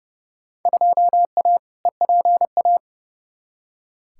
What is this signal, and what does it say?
Morse code, heard as a keyed tone, about 29 wpm, 700 Hz. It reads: 2A EPA
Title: Field Day Exchanges - 30wpm